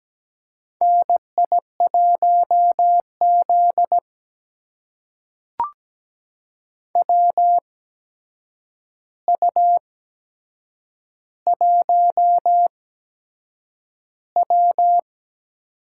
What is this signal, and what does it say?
Morse code, heard as a keyed tone, about 17 wpm, spaced as for 7 wpm, 700 Hz. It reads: NI1Z WU1W
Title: USA Call Signs - 4 Characters - 8x Character Spacing - 17wpm